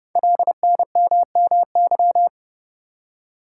Morse code, heard as a keyed tone, about 29 wpm, 700 Hz. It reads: LNMMY
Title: Code Groups - 5 Letters - 30wpm